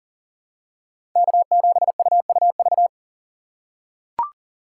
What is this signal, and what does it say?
Morse code, heard as a keyed tone, about 40 wpm, 700 Hz. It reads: K7UUV
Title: USA Call Signs - 2x Character Spacing - 40wpm